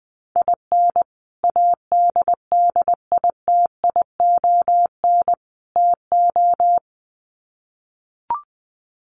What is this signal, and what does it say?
Morse code, heard as a keyed tone, about 20 wpm, 700 Hz. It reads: IN ADDITION TO
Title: Sets of 3 Words 20wpm